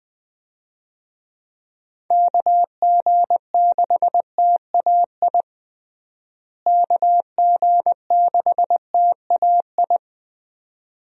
Morse code, heard as a keyed tone, about 20 wpm, 700 Hz. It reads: KG6TAI KG6TAI
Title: USA Call Signs - 6 Characters - Repeated 2x - 20wpm